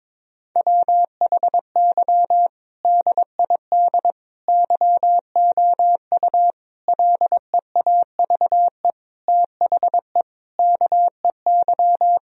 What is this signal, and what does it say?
Morse code, heard as a keyed tone, about 22 wpm, 700 Hz. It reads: WHY DID YOU LEAVE THE KEY
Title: Sentences from Top 1000 Words - Review Part 4 - 22wpm